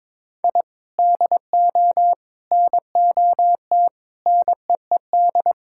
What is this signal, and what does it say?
Morse code, heard as a keyed tone, about 21 wpm, 700 Hz. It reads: I DO NOT NEED
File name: practice / 22wpm / Build A Phrase - 5-Word Phrases - 22wpm